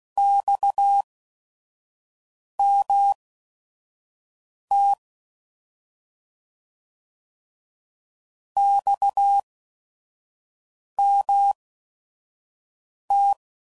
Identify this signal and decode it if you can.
Morse code, heard as a keyed tone, about 16 wpm, spaced as for 5 wpm, 800 Hz. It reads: XMT XMT
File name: Mixed / Abreviations / Abreviations